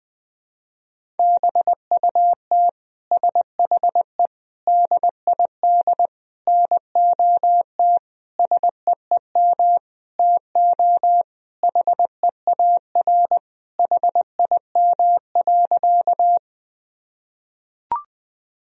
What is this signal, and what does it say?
Morse code, heard as a keyed tone, about 20 wpm, 700 Hz. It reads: BUT SHE DID NOT SEEM TO HEAR HIM.
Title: Sentences from Top 1000 Words - Review Part 1 - 20wpm